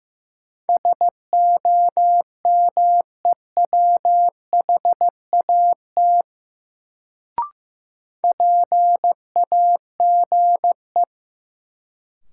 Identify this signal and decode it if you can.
Morse code, heard as a keyed tone, about 15 wpm, 700 Hz. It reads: SOMEWHAT PAGE
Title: Sentences from Top 1000 Words - Review Part 1 - New Word Drill - 15wpm